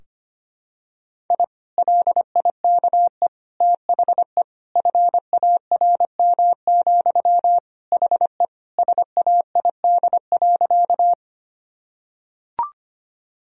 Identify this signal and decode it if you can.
Morse code, heard as a keyed tone, about 25 wpm, 700 Hz. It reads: I LIKE THE FARM, HE SAID.
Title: Sentences from Top 1000 Words - Review Part 2 - 25wpm